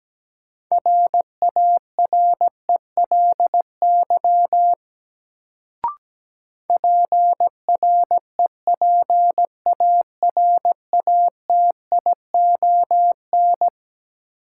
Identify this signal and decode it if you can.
Morse code, heard as a keyed tone, about 17 wpm, 700 Hz. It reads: RARELY PREPARATION
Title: Sentences from Top 1400 Words - New Word Drill - 17wpm